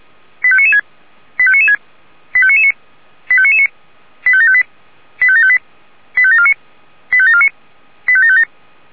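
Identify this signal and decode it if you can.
Five-tone selective call (ZVEI), heard as quick runs of five tones, tones of about 70 ms. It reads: 85996 85996 85999 85999 85758 85758 85749 85749 85757
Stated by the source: ZVEI five-tone selective call received on an SDR.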